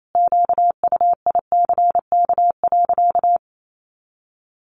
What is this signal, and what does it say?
Morse code, heard as a keyed tone, about 28 wpm, 700 Hz. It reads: QUICK.
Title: Sentences from Top 500 Words 28wpm